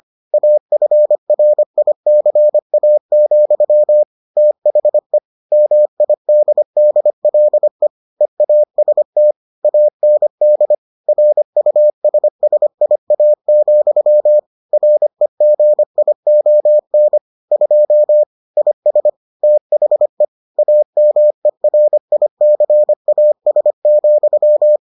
Morse code, heard as a keyed tone, about 25 wpm, 600 Hz. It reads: AFRICA, THE MIDDLE EAST AND RUSSIA, REGION 2 IS THE AMERICAS,